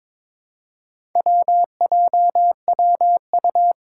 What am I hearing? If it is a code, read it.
Morse code, heard as a keyed tone, about 22 wpm, 700 Hz. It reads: WJWU
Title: Four Challenging Letter-Number Combinations 22wpm